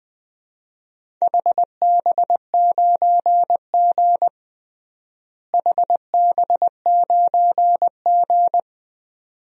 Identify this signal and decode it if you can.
Morse code, heard as a keyed tone, about 20 wpm, 700 Hz. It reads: HB9G HB9G
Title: DXCC-100 Call Signs - 4 Characters - Repeated 2x - 20wpm